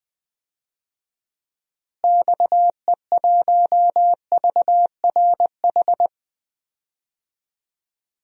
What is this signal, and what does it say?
Morse code, heard as a keyed tone, about 20 wpm, 700 Hz. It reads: XE1VRH
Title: DXCC-100 Call Signs - 6 Characters - Repeated 3x - 20wpm